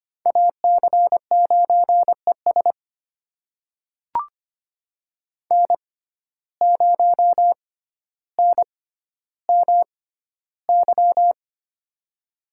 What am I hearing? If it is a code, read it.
Morse code, heard as a keyed tone, about 25 wpm, spaced as for 13 wpm, 700 Hz. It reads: AC9ES N0NMY
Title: USA Call Signs - 5 Characters - 6x Character Spacing - 25wpm